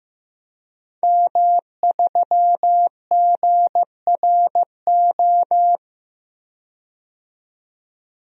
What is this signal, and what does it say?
Morse code, heard as a keyed tone, about 15 wpm, 700 Hz. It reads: M3GRO
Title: DXCC-100 Call Signs - 5 Characters - 15wpm